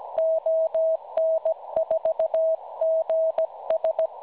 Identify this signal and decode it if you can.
Morse code, heard as a keyed tone, about 16 wpm, 650 Hz. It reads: ON4GS
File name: cwronde / websdr recording .6kHz